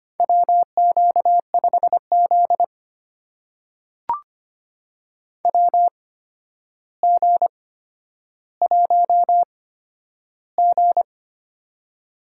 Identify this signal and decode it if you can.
Morse code, heard as a keyed tone, about 25 wpm, spaced as for 11 wpm, 700 Hz. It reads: WQ5Z WG1G
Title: USA Call Signs - 4 Characters - 8x Character Spacing - 25wpm